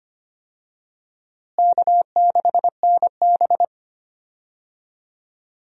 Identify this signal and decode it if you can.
Morse code, heard as a keyed tone, about 25 wpm, 700 Hz. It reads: K6NB
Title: USA Call Signs - 4 Characters - Repeated 2x - 25wpm